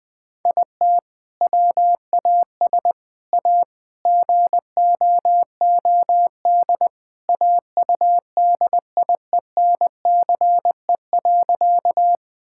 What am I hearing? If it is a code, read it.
Morse code, heard as a keyed tone, about 20 wpm, 700 Hz. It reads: IT WAS A GOOD AUDIENCE.
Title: Sentences from Top 1000 Words - Review Part 5 - 20wpm